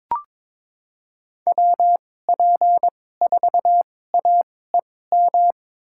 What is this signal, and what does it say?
Morse code, heard as a keyed tone, about 22 wpm, spaced as for 16 wpm, 700 Hz. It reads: WP4AEM
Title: USA Call Signs - 2x Character Spacing - 22wpm